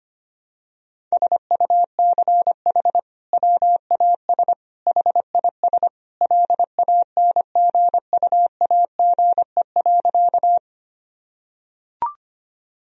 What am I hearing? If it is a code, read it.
Morse code, heard as a keyed tone, about 25 wpm, 700 Hz. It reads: SUCH WAS HIS LANGUAGE.
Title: Sentences from Top 1000 Words - Review Part 3 - 25wpm